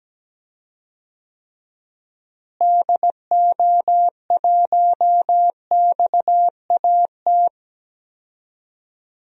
Morse code, heard as a keyed tone, about 17 wpm, 700 Hz. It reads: DO1XAT
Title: DXCC-100 Call Signs - 6 Characters - Repeated 2x - 17wpm